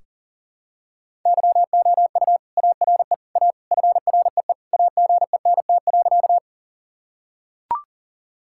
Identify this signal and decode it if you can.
Morse code, heard as a keyed tone, about 40 wpm, 700 Hz. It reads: YOU ARE A FREE AGENT.
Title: Sentences from Top 1600 Words 40wpm